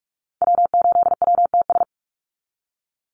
Morse code, heard as a keyed tone, about 47 wpm, 700 Hz. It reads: W8WTS
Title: Call Signs Repeated 2x - 45wpm